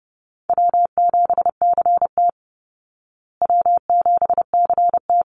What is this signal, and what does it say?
Morse code, heard as a keyed tone, about 30 wpm, 700 Hz. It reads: W7CT W7CT
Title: Easy Call Signs Repeated 2x - 30wpm